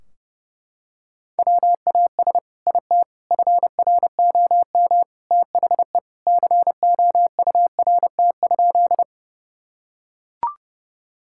Morse code, heard as a keyed tone, about 32 wpm, 700 Hz. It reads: WAS IT FROM THE COURT?
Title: Sentences from Top 1600 Words 30wpm